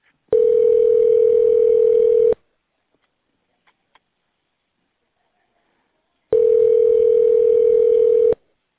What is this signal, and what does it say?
Ringback tone (North America), 440 + 480 Hz.